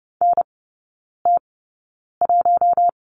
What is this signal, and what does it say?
Morse code, heard as a keyed tone, about 30 wpm, spaced as for 9 wpm, 700 Hz. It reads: NT1
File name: Lesson 17e - Letter u - Three Characters 30wpm